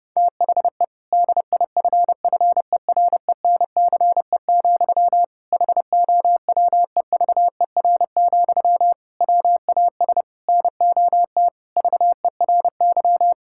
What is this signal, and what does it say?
Morse code, heard as a keyed tone, about 30 wpm, 700 Hz. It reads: THE DIFFERENCE, HOWEVER, WAS NOT VERY
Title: Sentences from Top 1000 Words - Review Part 5 - 30wpm